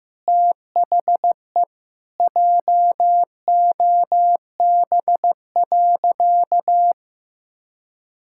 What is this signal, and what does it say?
Morse code, heard as a keyed tone, about 16 wpm, 700 Hz. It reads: THE JOB.
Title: Sentences from Top 1600 Words 15wpm